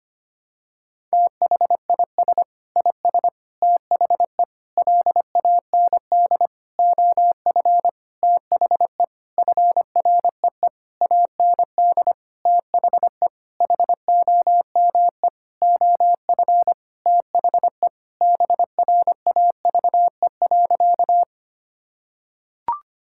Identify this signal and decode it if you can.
Morse code, heard as a keyed tone, about 25 wpm, 700 Hz. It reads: THIS IS THE LAND OF THE FREE AND THE HOME OF THE BRAVE.
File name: practice / 25wpm / Sentences with 9-Word Phrases - 25wpm